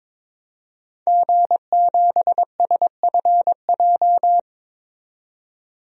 Morse code, heard as a keyed tone, about 22 wpm, 700 Hz. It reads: G7SFJ
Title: DXCC-100 Call Signs - 5 Characters - 22wpm